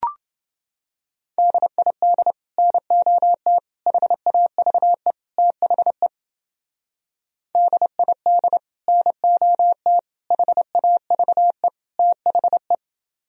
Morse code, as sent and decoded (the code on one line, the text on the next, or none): -.. .. -.. / -. --- - / .... .- ...- . / - .... / -.. .. -.. / -. --- - / .... .- ...- . / - .... .
DID NOT HAVE THE DID NOT HAVE THE